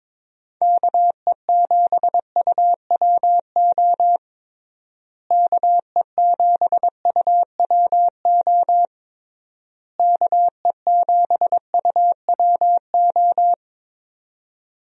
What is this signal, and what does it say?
Morse code, heard as a keyed tone, about 22 wpm, 700 Hz. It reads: KE7UWO KE7UWO KE7UWO
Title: USA Call Signs - 6 Characters - Repeated 3x - 22wpm